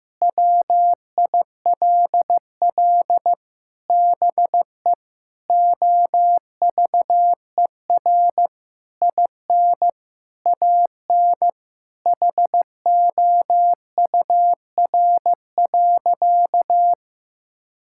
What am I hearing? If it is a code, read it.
Morse code, heard as a keyed tone, about 15 wpm, 700 Hz. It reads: WILL BE OVER IN AN HOUR.
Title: Sentences from Top 1000 Words - Review Part 5 - 15wpm